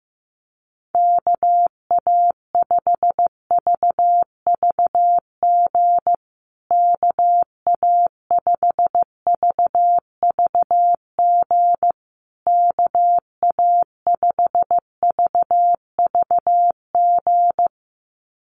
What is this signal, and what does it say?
Morse code, heard as a keyed tone, about 15 wpm, 700 Hz. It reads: KA5VVG KA5VVG KA5VVG
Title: USA Call Signs Repeated 3x 15wpm